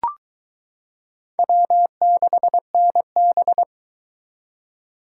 Morse code, heard as a keyed tone, about 22 wpm, 700 Hz. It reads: W6NB